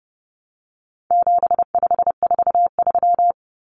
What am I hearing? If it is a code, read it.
Morse code, heard as a keyed tone, about 30 wpm, 700 Hz. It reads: 7543
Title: Common Cut Numbers - 4-Digit 30wpm